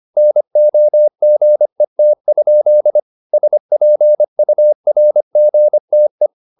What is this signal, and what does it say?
Morse code, heard as a keyed tone, about 25 wpm, 600 Hz. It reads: NOGET? SPURGTE
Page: Ferie 25 wpm